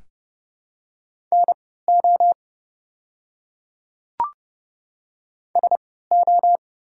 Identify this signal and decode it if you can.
Morse code, heard as a keyed tone, about 30 wpm, spaced as for 17 wpm, 700 Hz. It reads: NO SO
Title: 3x Character Spacing - Top 100 Words - 30wpm